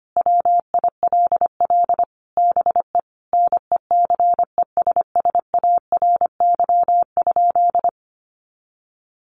Morse code, heard as a keyed tone, about 25 wpm, 700 Hz. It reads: WILL BE NECESSARY?
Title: Sentences from Top 800 Words 25wpm